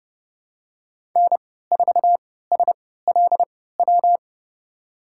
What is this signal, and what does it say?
Morse code, heard as a keyed tone, about 32 wpm, spaced as for 17 wpm, 700 Hz. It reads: N4SLW
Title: USA Call Signs - 5 Characters - 3x Character Spacing - 30wpm